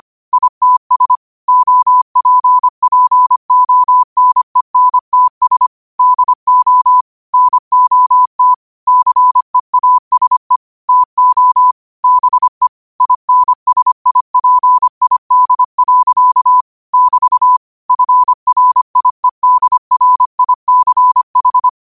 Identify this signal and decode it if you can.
Morse code, heard as a keyed tone, about 25 wpm, 1000 Hz. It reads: ITS OPPONENTS DO NOT CEASE TO BE INSIPID. <BT> FRIEDRICH